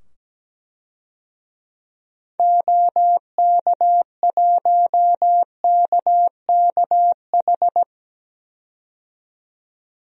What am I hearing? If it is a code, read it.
Morse code, heard as a keyed tone, about 17 wpm, 700 Hz. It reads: OK1KKH
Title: DXCC-100 Call Signs - 6 Characters - 17wpm